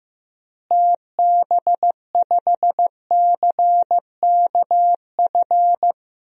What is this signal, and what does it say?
Morse code, heard as a keyed tone, about 15 wpm, 700 Hz. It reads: TB5CKF
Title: DXCC-100 Call Signs - 6 Characters - Repeated 2x - 15wpm